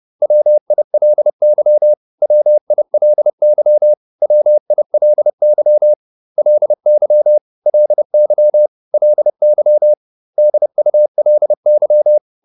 Morse code, heard as a keyed tone, about 30 wpm, 600 Hz. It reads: WILY WILY WILY LY LY LY DULY